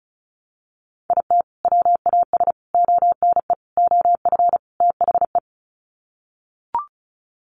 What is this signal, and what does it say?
Morse code, heard as a keyed tone, about 34 wpm, 700 Hz. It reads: IT WAS ONE OF THE
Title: Sets of 5 Words 35wpm